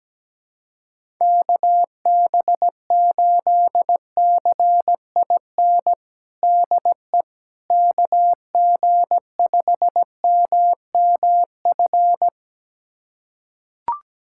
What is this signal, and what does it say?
Morse code, heard as a keyed tone, about 17 wpm, 700 Hz. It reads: KB8CIN DE KG5MMF
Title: USA Callsign DE Callsign - 6 Characters - 17wpm